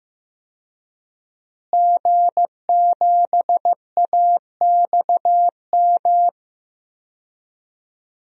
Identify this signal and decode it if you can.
Morse code, heard as a keyed tone, about 15 wpm, 700 Hz. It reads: G7AXM
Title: DXCC-100 Call Signs - 5 Characters - Repeated 2x - 15wpm